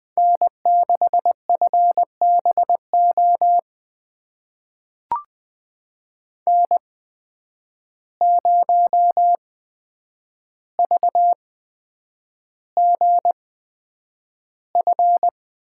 Morse code, heard as a keyed tone, about 20 wpm, spaced as for 9 wpm, 700 Hz. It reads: N6FBO N0VGF
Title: USA Call Signs - 5 Characters - 8x Character Spacing - 20wpm